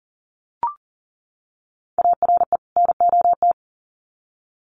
Morse code, heard as a keyed tone, about 40 wpm, 700 Hz. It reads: ARE NOT
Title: Sets of 2 Words 40wpm